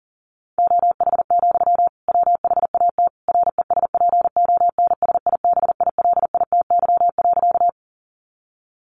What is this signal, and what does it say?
Morse code, heard as a keyed tone, about 40 wpm, 700 Hz. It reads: OH, WHAT RESPONSIBILITY.
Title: Sentences from Top 1200 Words 40wpm